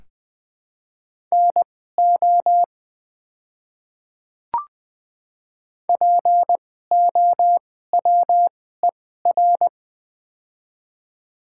Morse code, heard as a keyed tone, about 20 wpm, spaced as for 14 wpm, 700 Hz. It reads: NO POWER
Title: 2x Character Spacing - Top 500 Words - 20wpm